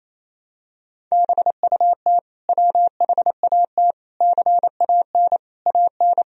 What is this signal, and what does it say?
Morse code, heard as a keyed tone, about 28 wpm, 700 Hz. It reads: BUT WHAT CAN AN
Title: Sentences from Top 1000 Words - Review Part 4 - 28wpm